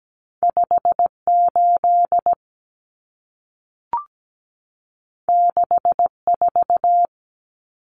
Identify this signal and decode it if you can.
Morse code, heard as a keyed tone, about 17 wpm, 700 Hz. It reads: 58 64
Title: Common Cut Numbers - 2-Digit 17wpm